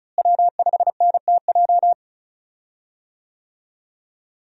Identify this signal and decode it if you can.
Morse code, heard as a keyed tone, about 36 wpm, 700 Hz. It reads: W5NTJ
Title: USA Call Signs - 5 Characters - Repeated 3x - 35wpm